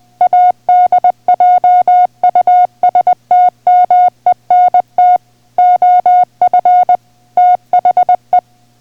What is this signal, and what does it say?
Morse code, heard as a keyed tone, about 20 wpm, 720 Hz. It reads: ADJUSTMENT OF THE